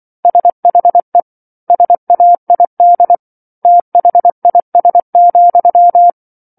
Morse code, heard as a keyed tone, about 24 wpm, 700 Hz. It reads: SHE SAID THIS,